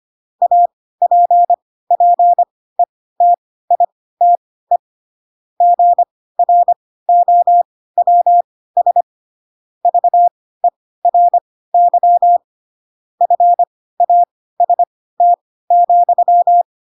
Morse code, heard as a keyed tone, about 25 wpm, spaced as for 16 wpm, 700 Hz. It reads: APPETITE GROWS VERY FAST,